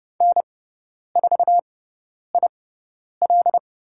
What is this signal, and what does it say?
Morse code, heard as a keyed tone, about 30 wpm, spaced as for 10 wpm, 700 Hz. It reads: N4IL
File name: Lesson 17g - Letter u - Call Signs 30wpm